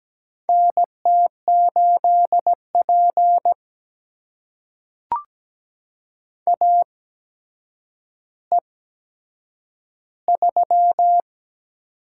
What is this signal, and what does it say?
Morse code, heard as a keyed tone, about 17 wpm, spaced as for 8 wpm, 700 Hz. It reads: NT8P AE3
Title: USA Call Signs - 4 Characters - 8x Character Spacing - 17wpm